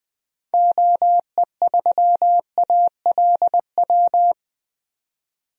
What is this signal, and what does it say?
Morse code, heard as a keyed tone, about 20 wpm, 700 Hz. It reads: OE3ALW
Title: DXCC-100 Call Signs - 6 Characters - 20wpm